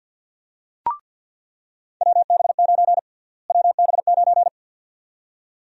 Morse code, heard as a keyed tone, about 49 wpm, 700 Hz. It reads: WB9 WB9
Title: USA Call Sign Prefixes - Repeated 2x - 50wpm